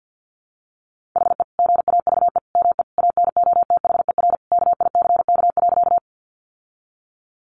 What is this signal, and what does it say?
Morse code, heard as a keyed tone, about 48 wpm, 700 Hz. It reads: HE GAVE ME ANOTHER KICK.
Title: Sentences from Top 1300 Words 50wpm